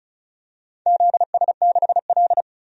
Morse code, heard as a keyed tone, about 36 wpm, 700 Hz. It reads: ZS6L
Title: DXCC-100 Call Signs - 4 Characters - 35wpm